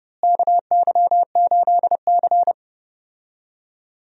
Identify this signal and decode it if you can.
Morse code, heard as a keyed tone, about 30 wpm, 700 Hz. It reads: KY8C
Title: USA Call Signs - 4 Characters - Repeated 3x - 30wpm